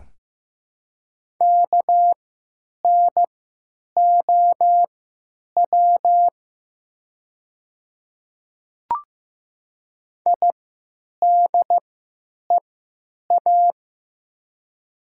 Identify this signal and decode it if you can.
Morse code, heard as a keyed tone, about 15 wpm, spaced as for 9 wpm, 700 Hz. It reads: KNOW IDEA
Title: 3x Character Spacing - Top 300 Words - 15wpm